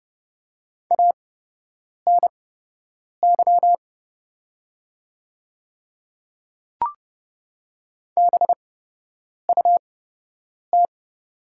Morse code, heard as a keyed tone, about 30 wpm, spaced as for 8 wpm, 700 Hz. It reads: ANY BUT
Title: 8x Character Spacing - Top 200 Words - 30wpm